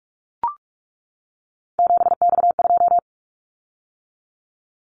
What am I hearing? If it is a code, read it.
Morse code, heard as a keyed tone, about 44 wpm, 700 Hz. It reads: 7X2
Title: Callsign Prefixes for Top 100 DXCC - 45wpm